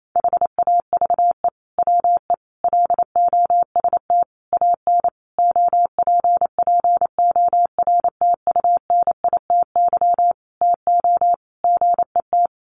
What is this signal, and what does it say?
Morse code, heard as a keyed tone, about 28 wpm, 700 Hz. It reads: HAVE WE LOST AN OPPORTUNITY TO GET
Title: Sentences from Top 1400 Words 28wpm